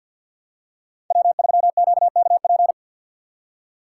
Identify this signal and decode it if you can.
Morse code, heard as a keyed tone, about 52 wpm, 700 Hz. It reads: W3QKP
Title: USA Call Signs - 3x Character Spacing - 50wpm